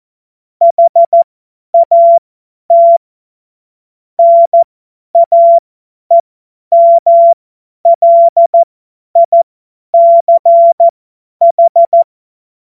Morse code, heard as a keyed tone, about 14 wpm, spaced as for 10 wpm, 680 Hz. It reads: HAT NAEMLICH
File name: Morsekurs / Witze / Witz 02. August 2022 50 F70